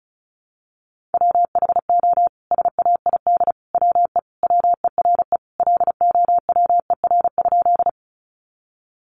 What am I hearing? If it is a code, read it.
Morse code, heard as a keyed tone, about 34 wpm, 700 Hz. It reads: WHO SAID WE WERE LOWER?
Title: Sentences from Top 700 Words 35wpm